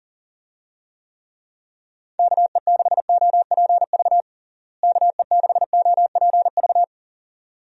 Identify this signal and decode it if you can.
Morse code, heard as a keyed tone, about 40 wpm, 700 Hz. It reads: KE6OPV KE6OPV
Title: USA Call Signs - 6 Characters - Repeated 2x - 40wpm